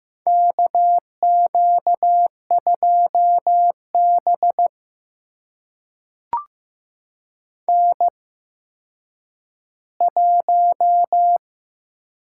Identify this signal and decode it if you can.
Morse code, heard as a keyed tone, about 15 wpm, spaced as for 9 wpm, 700 Hz. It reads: KQ2B N1
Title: USA Call Signs - 4 Characters - 8x Character Spacing - 15wpm